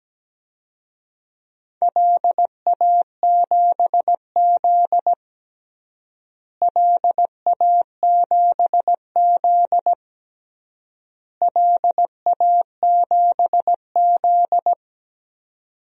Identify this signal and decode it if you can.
Morse code, heard as a keyed tone, about 17 wpm, 700 Hz. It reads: LA7Z LA7Z LA7Z
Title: DXCC-100 Call Signs - 4 Characters - Repeated 3x - 17wpm